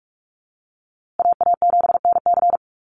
Morse code, heard as a keyed tone, about 44 wpm, 700 Hz. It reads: AA7NC
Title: USA Call Signs Repeated 3x 45wpm